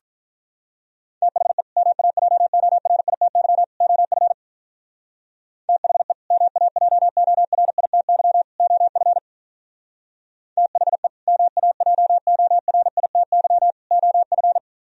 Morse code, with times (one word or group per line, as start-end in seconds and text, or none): 1.2-1.6 THE
1.8-3.6 MAJORITY
3.8-4.3 OF
5.7-6.1 THE
6.3-8.4 MAJORITY
8.6-9.2 OF
10.6-11.1 THE
11.3-13.7 MAJORITY
13.9-14.6 OF